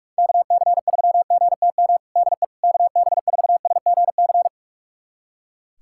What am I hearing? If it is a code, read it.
Morse code, heard as a keyed tone, about 41 wpm, 700 Hz. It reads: KC3GTM DE KB4SGC
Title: USA Callsign DE Callsign - 6 Characters - 45wpm